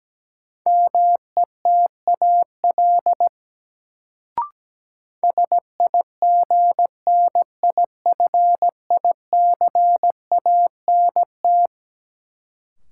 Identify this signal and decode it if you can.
Morse code, heard as a keyed tone, about 17 wpm, 700 Hz. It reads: METAL SIGNIFICANT
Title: Sentences from Top 1600 Words - New Word Drill Alt - 17wpm